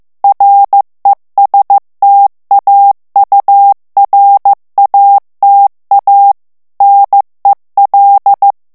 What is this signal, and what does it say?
Morse code, heard as a keyed tone, about 15 wpm, 800 Hz. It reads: RESTAURATA NEL
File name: morse mp3 / ITALIEN texte clair 1 vitesse 14 mots minute